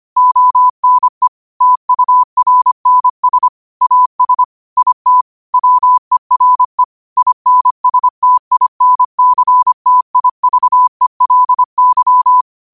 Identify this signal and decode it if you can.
Morse code, heard as a keyed tone, about 25 wpm, 1000 Hz. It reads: ONE TURNS AS IT WERE INSTINCTIVELY
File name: qotdfiles / 25wpm